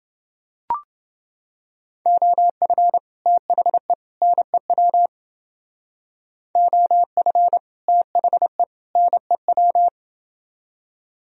Morse code, heard as a keyed tone, about 29 wpm, 700 Hz. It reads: OF THE NEW OF THE NEW